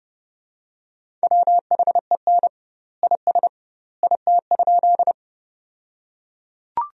Morse code, heard as a keyed tone, about 30 wpm, 700 Hz. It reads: WHEN IS IT?
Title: 2x Word-Spacing - Sentences from Top 100 Words - 30wpm